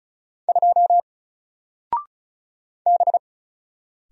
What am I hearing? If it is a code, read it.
Morse code, heard as a keyed tone, about 36 wpm, 700 Hz. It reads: 2 B
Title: Single Character (Letter:Number:Punctuation:Prosign) 35wpm